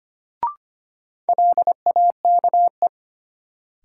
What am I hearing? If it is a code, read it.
Morse code, heard as a keyed tone, about 25 wpm, 700 Hz. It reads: LAKE
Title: Sentences from Top 1000 Words - Review Part 3 - New Word Drill - 25wpm